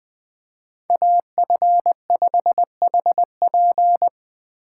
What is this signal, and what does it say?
Morse code, heard as a keyed tone, about 20 wpm, 700 Hz. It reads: AF5HP